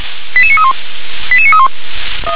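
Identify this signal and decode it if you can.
Five-tone selective call (ZVEI), heard as quick runs of five tones, tones of about 70 ms. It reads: 88931 88931
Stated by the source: ZVEI five-tone selective call received on an SDR.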